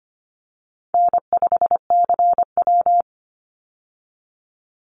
Morse code, heard as a keyed tone, about 25 wpm, 700 Hz. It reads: N5CW
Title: Easy Call Signs - 25wpm